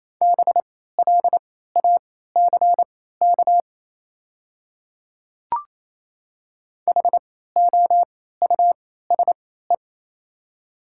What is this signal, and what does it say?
Morse code, heard as a keyed tone, about 28 wpm, spaced as for 16 wpm, 700 Hz. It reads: BLACK HOUSE
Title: 3x Character Spacing - Top 500 Words - 28wpm